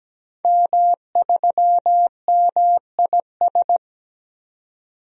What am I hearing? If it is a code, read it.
Morse code, heard as a keyed tone, about 17 wpm, 700 Hz. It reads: M3MIS
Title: DXCC-100 Call Signs - 5 Characters - 17wpm